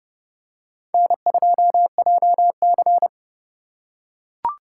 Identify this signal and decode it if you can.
Morse code, heard as a keyed tone, about 29 wpm, 700 Hz. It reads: N2JC